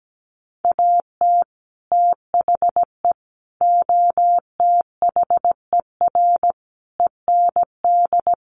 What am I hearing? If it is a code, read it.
Morse code, heard as a keyed tone, about 17 wpm, 700 Hz. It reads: AT THE OTHER END
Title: Sets of 5 Words 17wpm